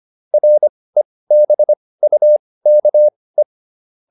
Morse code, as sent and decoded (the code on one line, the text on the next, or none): .-. -... ..- -.- .
REBUKE